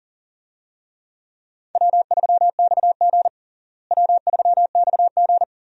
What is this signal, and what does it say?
Morse code, heard as a keyed tone, about 37 wpm, 700 Hz. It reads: W3XG W3XG
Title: USA Call Signs - 4 Characters - Repeated 2x - 40wpm